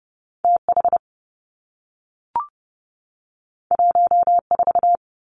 Morse code, heard as a keyed tone, about 30 wpm, 700 Hz. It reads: T H 1 4
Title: Two Challenging Letter-Number Combinations 30wpm